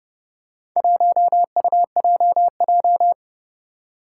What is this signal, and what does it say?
Morse code, heard as a keyed tone, about 30 wpm, 700 Hz. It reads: 1UJJ
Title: Four Challenging Letter-Number Combinations 30wpm